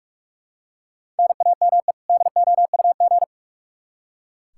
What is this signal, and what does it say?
Morse code, heard as a keyed tone, about 44 wpm, 700 Hz. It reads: NAME DOUG
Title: Practice QSO - Exchange 1 - Beginning - 45wpm